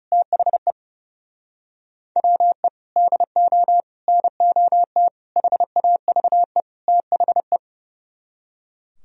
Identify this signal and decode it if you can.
Morse code, heard as a keyed tone, about 39 wpm, 700 Hz. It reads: THE WE DO NOT HAVE THE